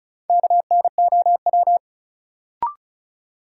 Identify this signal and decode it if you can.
Morse code, heard as a keyed tone, about 34 wpm, 700 Hz. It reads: KNOW
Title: Sentences from Top 100 Words - New Word Drill - 35wpm